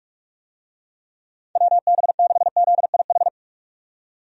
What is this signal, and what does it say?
Morse code, heard as a keyed tone, about 47 wpm, 700 Hz. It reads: WB6ZIH
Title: USA Call Signs - 6 Characters - 45wpm